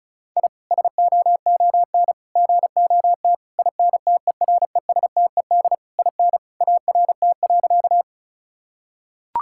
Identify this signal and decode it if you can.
Morse code, heard as a keyed tone, about 34 wpm, 700 Hz. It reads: I SOON GOT INTERESTED IN ART.
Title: Sentences from Top 1000 Words - Review Part 6 - 35wpm